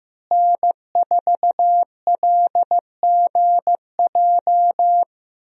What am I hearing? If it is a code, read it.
Morse code, heard as a keyed tone, about 15 wpm, 700 Hz. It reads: N4LGJ
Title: USA Call Signs - 5 Characters - Repeated 3x - 15wpm